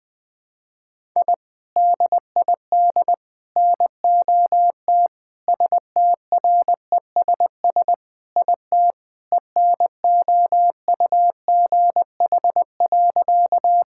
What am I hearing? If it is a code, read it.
Morse code, heard as a keyed tone, about 21 wpm, 700 Hz. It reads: I DID NOT STRESS IT ENOUGH.
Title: Sentences from Top 1000 Words - Review Part 6 - 20wpm